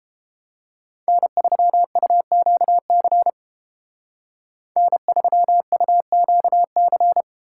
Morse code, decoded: N3UQC N3UQC

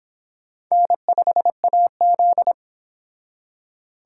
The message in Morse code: N5AZ

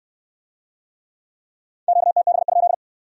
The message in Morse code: QTB?

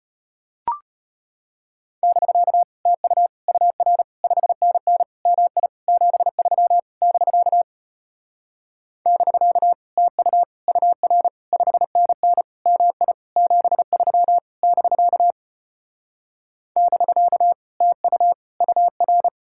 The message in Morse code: <BK> TU UR 5NN MI 73 <BK> <BK> TU UR 5NN MI 73 <BK> <BK> TU UR

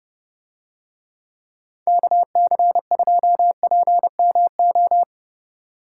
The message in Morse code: KC2PMO

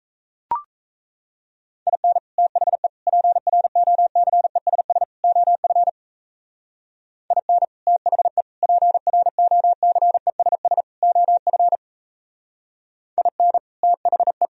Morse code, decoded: IN THE PROCESS OF IN THE PROCESS OF IN THE

